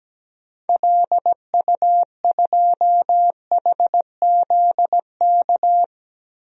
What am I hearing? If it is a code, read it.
Morse code, heard as a keyed tone, about 17 wpm, 700 Hz. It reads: LU2HZK
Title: DXCC-100 Call Signs - 6 Characters - Repeated 3x - 17wpm